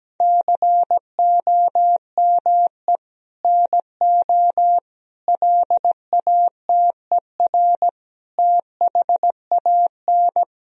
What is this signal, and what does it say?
Morse code, heard as a keyed tone, about 17 wpm, 700 Hz. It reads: COME NO LATER THAN
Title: Sentences from Top 1000 Words - Review Part 4 - 17wpm